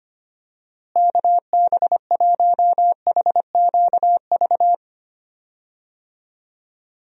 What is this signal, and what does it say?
Morse code, heard as a keyed tone, about 25 wpm, 700 Hz. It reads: KB1HQV
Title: USA Call Signs - 6 Characters - Repeated 2x - 25wpm